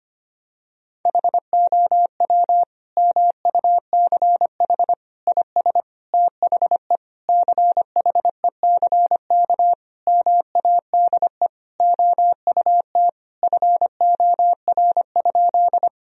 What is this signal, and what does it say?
Morse code, heard as a keyed tone, about 25 wpm, 700 Hz. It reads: HOW MUCH IS THE CHECK MADE OUT FOR?
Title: Sentences from Top 1000 Words - Review Part 3 - 25wpm